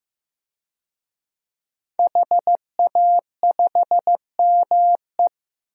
Morse code, heard as a keyed tone, about 15 wpm, 700 Hz. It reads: HA5ME
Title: DXCC-100 Call Signs - 5 Characters - Repeated 3x - 15wpm